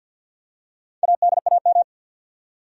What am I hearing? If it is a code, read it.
Morse code, heard as a keyed tone, about 53 wpm, 700 Hz. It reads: ADAM
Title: USA Common Names - Max 4 Letters - 50wpm